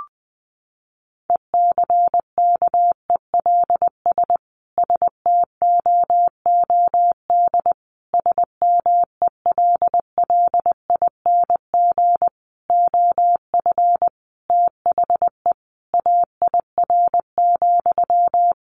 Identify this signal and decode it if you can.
Morse code, heard as a keyed tone, about 20 wpm, 700 Hz. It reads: ECKELS STOOD SMELLING OF THE AIR,
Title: A Sound Of Thunder 20wpm